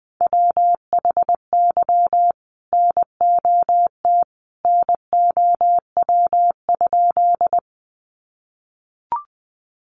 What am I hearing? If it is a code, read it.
Morse code, heard as a keyed tone, about 20 wpm, 700 Hz. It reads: WHY NOT NOW?
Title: Sentences from Top 600 Words 20wpm